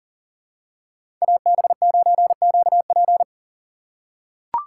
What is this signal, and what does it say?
Morse code, heard as a keyed tone, about 40 wpm, 700 Hz. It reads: AB9QP
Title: USA Call Signs - 5 Characters - 4x Character Spacing - 40wpm